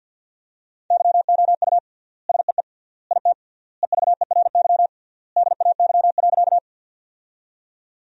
Morse code, heard as a keyed tone, about 50 wpm, 700 Hz. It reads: YOU SEE IT EVERY DAY.
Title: 3x Word-Spacing - Sentences from Top 200 Words - 50wpm